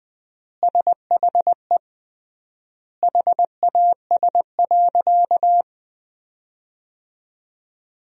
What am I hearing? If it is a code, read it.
Morse code, heard as a keyed tone, about 21 wpm, 700 Hz. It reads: SHE HAS.
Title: 3x Word-Spacing - Sentences from Top 500 Words - 20wpm